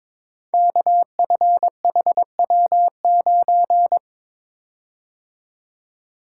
Morse code, heard as a keyed tone, about 22 wpm, 700 Hz. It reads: KFHW9
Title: Code Groups - 5 Letter:Numbers - 22wpm